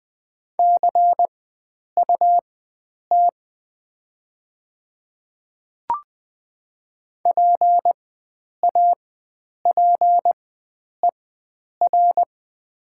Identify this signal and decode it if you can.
Morse code, heard as a keyed tone, about 20 wpm, spaced as for 9 wpm, 700 Hz. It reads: CUT PAPER
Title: 4x Character Spacing - Top 300 Words - 20wpm